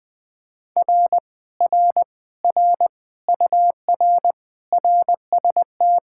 Morse code, heard as a keyed tone, about 20 wpm, 700 Hz. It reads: R R R UR RST
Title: Speed-Racing Common QSO Phrases - 17wpm